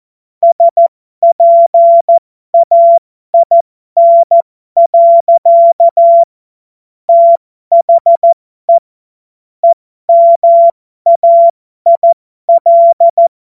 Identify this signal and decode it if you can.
Morse code, heard as a keyed tone, about 14 wpm, spaced as for 12 wpm, 680 Hz. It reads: SPAIN. THE EMAIL
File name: Morsekurs / Witze